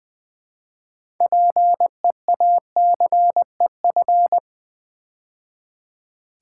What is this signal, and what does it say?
Morse code, heard as a keyed tone, about 20 wpm, 700 Hz. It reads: PEACEF
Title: Build A Word - 3 Letters At A Time - 8-Letter Words - 20wpm